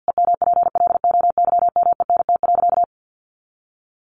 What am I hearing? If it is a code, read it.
Morse code, heard as a keyed tone, about 50 wpm, 700 Hz. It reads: EMPLOYMENT.
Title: Sentences from Top 1500 Words 50wpm